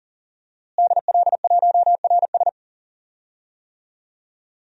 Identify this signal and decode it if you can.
Morse code, heard as a keyed tone, about 40 wpm, 700 Hz. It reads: DL1RS
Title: DXCC-100 Call Signs - 5 Characters - Repeated 3x - 40wpm